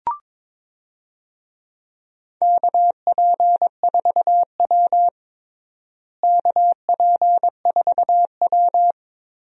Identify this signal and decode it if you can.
Morse code, heard as a keyed tone, about 22 wpm, 700 Hz. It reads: KP4W KP4W
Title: DXCC-100 Call Signs - 4 Characters - Repeated 3x - 22wpm